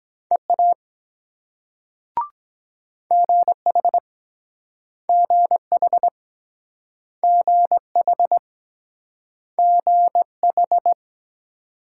Morse code, heard as a keyed tone, about 23 wpm, 700 Hz. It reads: EA GH GH GH GH